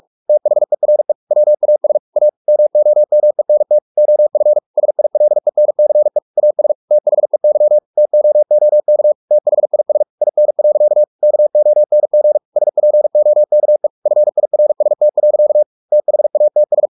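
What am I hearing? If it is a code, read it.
Morse code, heard as a keyed tone, about 44 wpm, 600 Hz. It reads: THERE WAS A MOMENT OF SILENCE AS THEY TOOK THIS IN. KONG SPOKE FIRST. THATS
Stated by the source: Morse practice at 45 wpm, the novel Never by Ken Follett in English, from CW med Gnister.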